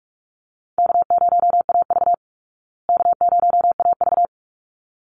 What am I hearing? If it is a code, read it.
Morse code, heard as a keyed tone, about 44 wpm, 700 Hz. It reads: K0AV K0AV
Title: Easy Call Signs Repeated 2x - 45wpm